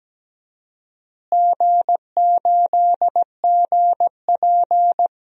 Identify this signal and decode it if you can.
Morse code, heard as a keyed tone, about 17 wpm, 700 Hz. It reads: G8GP
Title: DXCC-100 Call Signs - 4 Characters - Repeated 2x - 17wpm